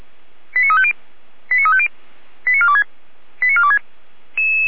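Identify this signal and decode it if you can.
Five-tone selective call (ZVEI), heard as quick runs of five tones, tones of about 70 ms. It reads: 89350 89350 89426 89426
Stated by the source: Received on an SDR, ZVEI five-tone call 89108, from the feuerwehr-orlamuende.org audio archive (Feuerwehr Orlamünde).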